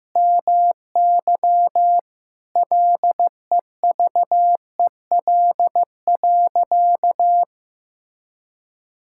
Morse code, heard as a keyed tone, about 16 wpm, 700 Hz. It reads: MY LEVEL.
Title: Sentences from Top 1000 Words - Review Part 6 - 15wpm